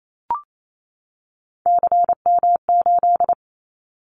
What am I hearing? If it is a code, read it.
Morse code, heard as a keyed tone, about 28 wpm, 700 Hz. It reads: CM8
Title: Callsign Prefixes for Top 100 DXCC - 28wpm